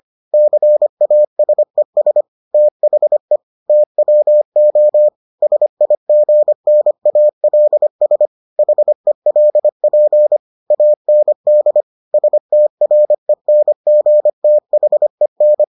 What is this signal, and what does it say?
Morse code, heard as a keyed tone, about 25 wpm, 600 Hz. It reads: CASES THE TWO SIGNALS HELP AND STRENGTHEN